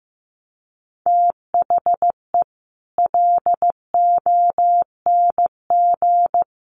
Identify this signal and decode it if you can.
Morse code, heard as a keyed tone, about 15 wpm, 700 Hz. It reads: THE LONG
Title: Sentences from Top 1200 Words 15wpm